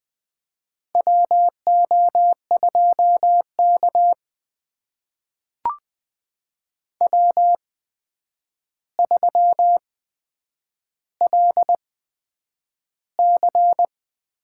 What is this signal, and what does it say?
Morse code, heard as a keyed tone, about 20 wpm, spaced as for 9 wpm, 700 Hz. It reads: WO2K W3LC
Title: USA Call Signs - 4 Characters - 8x Character Spacing - 20wpm